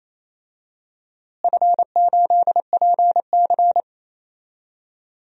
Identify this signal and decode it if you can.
Morse code, heard as a keyed tone, about 28 wpm, 700 Hz. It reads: F8PC
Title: DXCC-100 Call Signs - 4 Characters - Repeated 3x - 28wpm